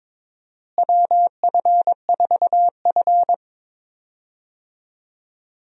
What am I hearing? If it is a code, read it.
Morse code, heard as a keyed tone, about 22 wpm, 700 Hz. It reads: WF4F
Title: USA Call Signs - 4 Characters - Repeated 3x - 22wpm